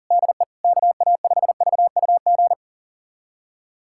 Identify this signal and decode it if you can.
Morse code, heard as a keyed tone, about 40 wpm, 700 Hz. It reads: DE KA5VUG
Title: USA Callsign DE Callsign - 6 Characters - 40wpm